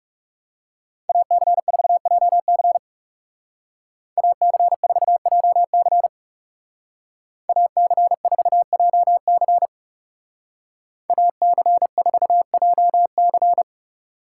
Morse code, heard as a keyed tone, about 39 wpm, 700 Hz. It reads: AC4JC AC4JC AC4JC AC4JC